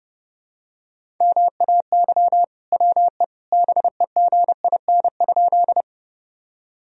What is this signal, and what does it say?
Morse code, heard as a keyed tone, about 30 wpm, 700 Hz. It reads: MAY WE BEGIN?
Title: Sentences from Top 1000 Words - Review Part 1 - 30wpm